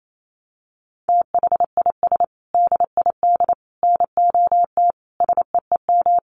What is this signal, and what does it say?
Morse code, heard as a keyed tone, about 28 wpm, 700 Hz. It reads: THIS DID NOT SEEM
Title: Sentences from Top 1400 Words 28wpm